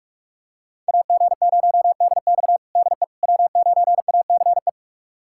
Morse code, heard as a keyed tone, about 41 wpm, 700 Hz. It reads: AG0DX DE W9ACE
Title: USA Callsign DE Callsign - 5 Characters - 45wpm